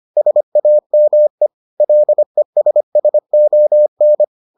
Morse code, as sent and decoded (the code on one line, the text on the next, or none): ... .- -- . / .-.. ... ... --- -.
SAME LESSON